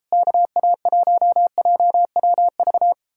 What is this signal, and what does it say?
Morse code, heard as a keyed tone, about 38 wpm, 700 Hz. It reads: KA1JWV